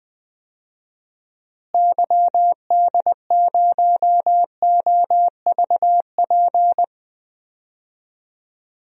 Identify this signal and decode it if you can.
Morse code, heard as a keyed tone, about 20 wpm, 700 Hz. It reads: YD0OVP
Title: DXCC-100 Call Signs - 6 Characters - Repeated 3x - 20wpm